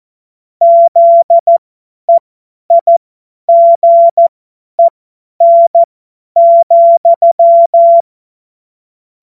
Morse code, heard as a keyed tone, about 14 wpm, spaced as for 10 wpm, 680 Hz. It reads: ZEIGEN,
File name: Morsekurs / Witze / Witz 23 Januar 2024 50 F70